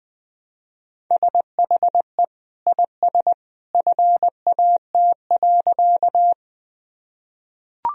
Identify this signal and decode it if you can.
Morse code, heard as a keyed tone, about 21 wpm, 700 Hz. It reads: SHE IS FAT.
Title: Sentences from Top 1000 Words - Review Part 4 - 20wpm